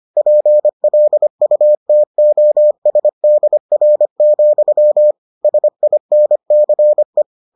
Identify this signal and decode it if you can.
Morse code, heard as a keyed tone, about 25 wpm, 600 Hz. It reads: PLUTOSDR, SINCE